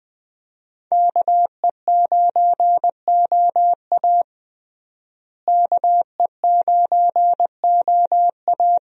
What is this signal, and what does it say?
Morse code, heard as a keyed tone, about 21 wpm, 700 Hz. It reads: KE9OA KE9OA
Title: USA Call Signs - 5 Characters - Repeated 3x - 20wpm